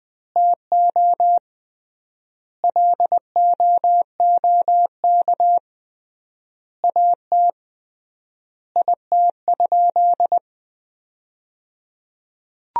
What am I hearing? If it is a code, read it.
Morse code, heard as a keyed tone, about 19 wpm, 700 Hz. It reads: TO LOOK AT IT?
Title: 3x Word-Spacing - Sentences from Top 100 Words - 20wpm